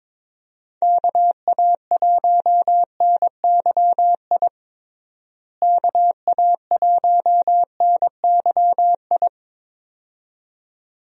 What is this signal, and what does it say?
Morse code, heard as a keyed tone, about 22 wpm, 700 Hz. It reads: KA1NYI KA1NYI
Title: USA Call Signs - 6 Characters - Repeated 2x - 22wpm